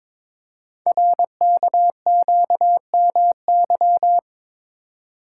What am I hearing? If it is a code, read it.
Morse code, heard as a keyed tone, about 23 wpm, 700 Hz. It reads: RKQMY
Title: Code Groups - 5 Letter:Numbers - 22wpm